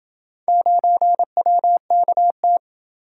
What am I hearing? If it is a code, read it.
Morse code, heard as a keyed tone, about 27 wpm, 700 Hz. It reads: 9WKT